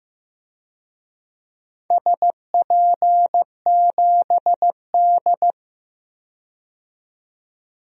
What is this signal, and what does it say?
Morse code, heard as a keyed tone, about 15 wpm, 700 Hz. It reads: SP7D
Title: DXCC-100 Call Signs - 4 Characters - 15wpm